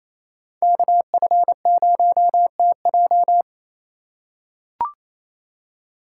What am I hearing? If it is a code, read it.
Morse code, heard as a keyed tone, about 28 wpm, 700 Hz. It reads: KF0TJ
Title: USA Call Signs - 5 Characters - 6x Character Spacing - 28wpm